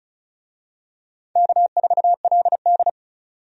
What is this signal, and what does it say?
Morse code, heard as a keyed tone, about 34 wpm, 700 Hz. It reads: K4LD
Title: USA Call Signs - 4 Characters - 35wpm